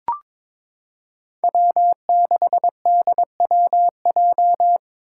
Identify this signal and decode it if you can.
Morse code, heard as a keyed tone, about 22 wpm, 700 Hz. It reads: W6DWJ
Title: Five Challenging Letter-Number Combinations 22wpm